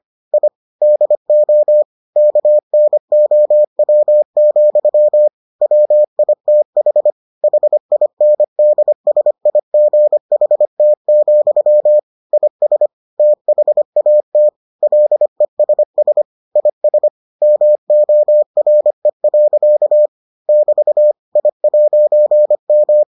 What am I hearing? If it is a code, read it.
Morse code, heard as a keyed tone, about 25 wpm, 600 Hz. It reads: I DO KNOW, WITH HINDSIGHT, IS THAT LESS IS MORE. <BT> I'M